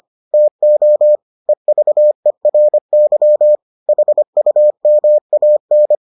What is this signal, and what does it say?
Morse code, heard as a keyed tone, about 25 wpm, 600 Hz. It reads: TO EVERY HUMAN